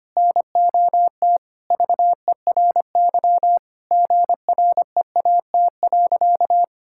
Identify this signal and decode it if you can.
Morse code, heard as a keyed tone, about 25 wpm, 700 Hz. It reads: NOT VERY GREAT.
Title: Sentences from Top 1000 Words - Review Part 5 - 25wpm